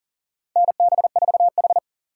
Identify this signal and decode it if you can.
Morse code, heard as a keyed tone, about 37 wpm, 700 Hz. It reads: NB4H
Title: USA Call Signs - 4 Characters - 4x Character Spacing - 40wpm